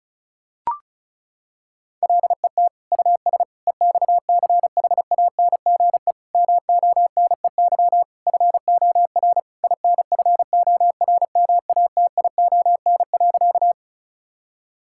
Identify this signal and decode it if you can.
Morse code, heard as a keyed tone, about 34 wpm, 700 Hz. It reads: LET US EXCHANGE MONEY FOR INFORMATION.
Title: Sentences from Top 1000 Words - Review Part 6 - 35wpm